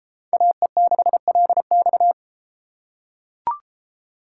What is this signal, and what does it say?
Morse code, heard as a keyed tone, about 32 wpm, 700 Hz. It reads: AE6LX